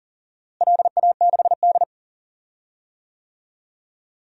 Morse code, heard as a keyed tone, about 40 wpm, 700 Hz. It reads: LA6D
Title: DXCC-100 Call Signs - 4 Characters - Repeated 2x - 40wpm